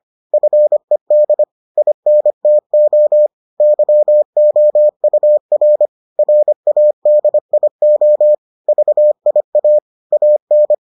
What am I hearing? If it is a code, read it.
Morse code, heard as a keyed tone, about 25 wpm, 600 Hz. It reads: FED INTO YOUR RADIO VIA AN